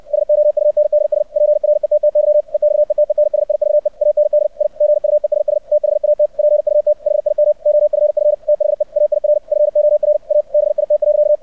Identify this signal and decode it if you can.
Morse code, heard as a keyed tone, about 64 wpm, 600 Hz. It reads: SOUNDS PRETTY EXETENSIVE AND I GUESS THAT PUT HIM OFF THE AIR FOR A BIT.
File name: archived / CRSnet